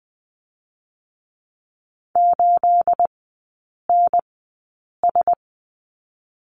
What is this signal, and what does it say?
Morse code, heard as a keyed tone, about 20 wpm, spaced as for 8 wpm, 700 Hz. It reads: 8NS
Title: 218 Lesson 35e - Number 0 - Three Characters 20wpm